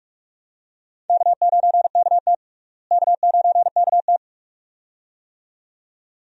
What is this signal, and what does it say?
Morse code, heard as a keyed tone, about 46 wpm, 700 Hz. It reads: K9KT K9KT
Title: USA Call Signs - 4 Characters - Repeated 2x - 45wpm